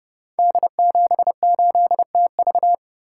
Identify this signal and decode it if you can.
Morse code, heard as a keyed tone, about 30 wpm, 700 Hz. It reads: D78TV
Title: Code Groups - 5 Letter:Numbers - 30wpm